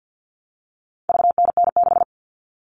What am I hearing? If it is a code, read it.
Morse code, heard as a keyed tone, about 50 wpm, 700 Hz. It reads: VNN6
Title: All Cut Numbers - 4-Digit 50wpm